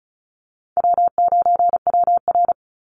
Morse code, heard as a keyed tone, about 34 wpm, 700 Hz. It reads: W9WR
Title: Call Signs Repeated 2x - 35wpm